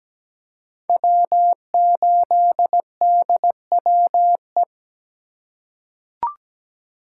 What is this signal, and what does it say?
Morse code, heard as a keyed tone, about 17 wpm, 700 Hz. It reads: W8DWE
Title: USA Call Signs - 5 Characters - 3x Character Spacing - 17wpm